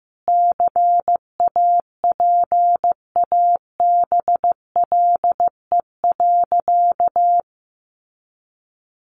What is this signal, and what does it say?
Morse code, heard as a keyed tone, about 14 wpm, 700 Hz. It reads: CAPABLE.
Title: Sentences from Top 1200 Words 15wpm